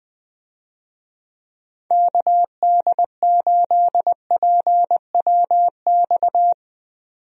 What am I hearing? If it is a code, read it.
Morse code, heard as a keyed tone, about 21 wpm, 700 Hz. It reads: KD8PWX
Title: USA Call Signs - 6 Characters - 20wpm